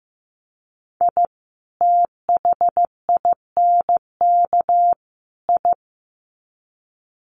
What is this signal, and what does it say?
Morse code, heard as a keyed tone, about 15 wpm, 700 Hz. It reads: I THINK I
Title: Sets of 3 Words 15wpm